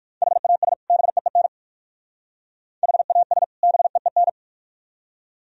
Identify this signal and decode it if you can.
Morse code, heard as a keyed tone, about 51 wpm, 700 Hz. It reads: HAS BEEN HAS BEEN
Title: Speed-Racing - Sets of 2 Words 45wpm